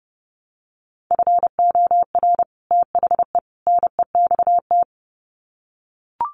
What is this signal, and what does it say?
Morse code, heard as a keyed tone, about 30 wpm, 700 Hz. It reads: FOR THE NEXT
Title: Sets of 3 Words 30wpm